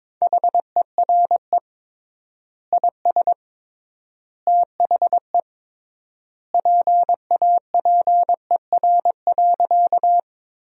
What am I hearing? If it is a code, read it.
Morse code, heard as a keyed tone, about 22 wpm, 700 Hz. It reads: HERE IS THE PAPER.
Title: 3x Word-Spacing - Sentences from Top 300 Words - 22wpm